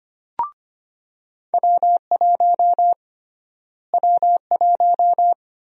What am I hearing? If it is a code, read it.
Morse code, heard as a keyed tone, about 25 wpm, 700 Hz. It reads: W1 W1
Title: USA Call Sign Prefixes - Repeated 2x - 25wpm